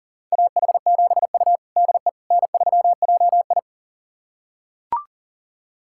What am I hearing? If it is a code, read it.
Morse code, heard as a keyed tone, about 40 wpm, 700 Hz. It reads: AH7U DE N3JI
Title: USA Callsign DE Callsign - 4 Characters - 40wpm